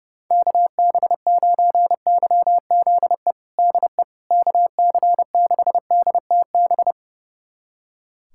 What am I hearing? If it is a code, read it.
Morse code, heard as a keyed tone, about 32 wpm, 700 Hz. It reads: KB9YZE DE KC6DTB
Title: USA Callsign DE Callsign - 6 Characters - 30wpm